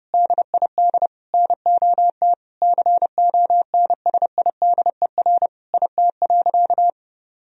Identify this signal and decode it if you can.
Morse code, heard as a keyed tone, about 32 wpm, 700 Hz. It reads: DID NOT CONSIDER IT.
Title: Sentences from Top 1000 Words - Review Part 4 - 30wpm